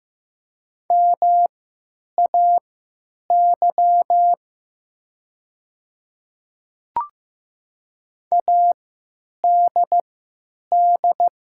Morse code, heard as a keyed tone, about 15 wpm, spaced as for 9 wpm, 700 Hz. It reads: MAY ADD
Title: 3x Character Spacing - Top 300 Words - 15wpm